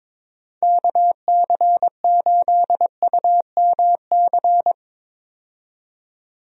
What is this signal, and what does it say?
Morse code, heard as a keyed tone, about 22 wpm, 700 Hz. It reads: KC8UMC
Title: USA Call Signs - 6 Characters - 22wpm